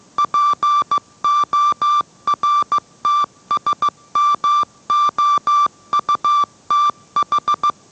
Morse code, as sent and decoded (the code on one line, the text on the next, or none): .--. --- .-. - ... -- --- ..- - ....
PORTSMOUTH